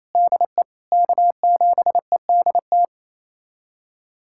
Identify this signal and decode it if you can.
Morse code, heard as a keyed tone, about 28 wpm, 700 Hz. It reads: DE K7EDT
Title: USA Callsign DE Callsign - 5 Characters - 28wpm